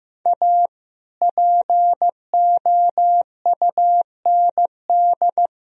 Morse code, heard as a keyed tone, about 15 wpm, 700 Hz. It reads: A POUND
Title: Sentences from Top 1000 Words - Review Part 2 - 15wpm